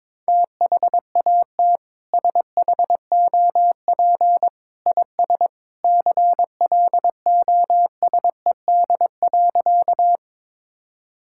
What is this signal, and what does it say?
Morse code, heard as a keyed tone, about 22 wpm, 700 Hz. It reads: THAT SHOP IS CLOSED.
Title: Sentences from Top 1600 Words 22wpm